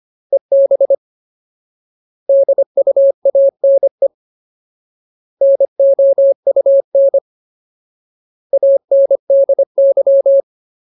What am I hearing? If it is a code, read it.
Morse code, heard as a keyed tone, about 25 wpm, 550 Hz. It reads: EB DUANE NOUN ANDY
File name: Ultimate Word List / 25WPM Ultimate